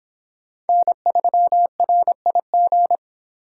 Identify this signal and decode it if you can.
Morse code, heard as a keyed tone, about 26 wpm, 700 Hz. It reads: N3RIG